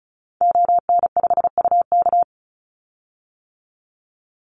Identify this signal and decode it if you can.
Morse code, heard as a keyed tone, about 34 wpm, 700 Hz. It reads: ON5UK
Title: Call Signs - 35wpm